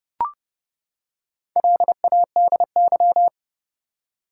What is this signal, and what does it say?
Morse code, heard as a keyed tone, about 32 wpm, 700 Hz. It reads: LADY
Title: Four Letter Words - 30wpm